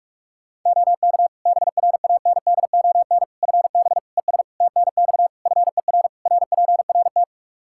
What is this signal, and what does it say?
Morse code, heard as a keyed tone, about 41 wpm, 700 Hz. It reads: OK BRANDON FB ES TNX FER RPRT
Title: Practice QSO - Exchange 2 - Info - 45wpm